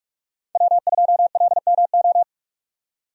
Morse code, heard as a keyed tone, about 41 wpm, 700 Hz. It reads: W2LMO
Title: USA Call Signs - 5 Characters - 45wpm